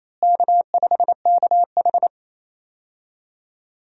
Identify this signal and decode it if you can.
Morse code, heard as a keyed tone, about 28 wpm, 700 Hz. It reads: K5KH
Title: USA Call Signs - 4 Characters - Repeated 2x - 28wpm